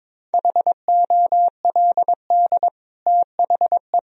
Morse code, heard as a keyed tone, about 22 wpm, 700 Hz. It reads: HOLD THE
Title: Sentences from Top 1000 Words - Review Part 2 - 22wpm